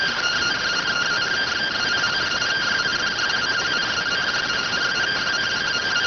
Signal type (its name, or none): two-tone teleprinter signal